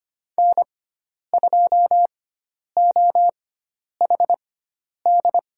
Morse code, heard as a keyed tone, about 25 wpm, spaced as for 10 wpm, 700 Hz. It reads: N2OHD
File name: Lesson 16g - Letter c - Call Signs 25wpm